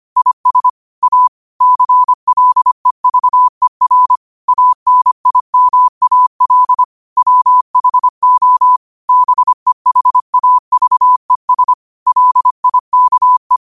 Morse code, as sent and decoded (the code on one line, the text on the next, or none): .. ... / .- / -.-. .-.. ...- . .-. / .- -. .. -- .- .-.. / .-- .... --- / -... .... .- ...- . ... / .-.. .. -.- .
IS A CLEVER ANIMAL WHO BEHAVES LIKE